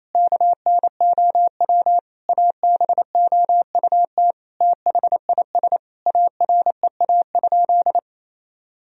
Morse code, heard as a keyed tone, about 28 wpm, 700 Hz. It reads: KNOW ABOUT THIS AREA?
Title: Sentences from Top 1000 Words - Review Part 2 - 28wpm